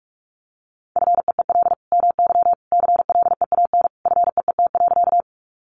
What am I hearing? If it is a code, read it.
Morse code, heard as a keyed tone, about 44 wpm, 700 Hz. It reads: FEEL MY CLEAN FEET.
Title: Sentences from Top 1100 Words 45wpm